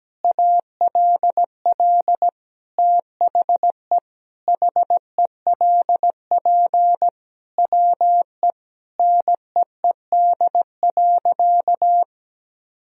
Morse code, heard as a keyed tone, about 17 wpm, 700 Hz. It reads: ALL THE HELP WE NEED.
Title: Sentences from Top 1000 Words - Review Part 1 - 17wpm